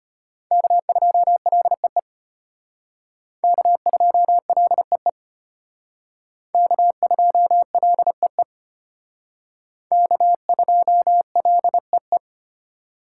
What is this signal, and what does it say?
Morse code, heard as a keyed tone, about 34 wpm, 700 Hz. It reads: K2LEE K2LEE K2LEE K2LEE